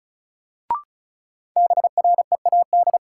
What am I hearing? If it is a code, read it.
Morse code, heard as a keyed tone, about 34 wpm, 700 Hz. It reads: BREAD
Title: Sentences from Top 1600 Words - New Word Drill Alt - 35wpm